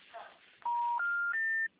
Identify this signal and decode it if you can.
Special information tone, with speech.